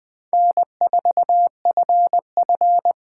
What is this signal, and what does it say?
Morse code, heard as a keyed tone, about 24 wpm, 700 Hz. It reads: N4FF
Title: Speed Racing - USA 1x2 Call Signs 20wpm